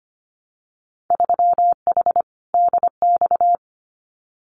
Morse code, heard as a keyed tone, about 25 wpm, 700 Hz. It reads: 3H DX
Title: Winter Field Day Exchanges 25wpm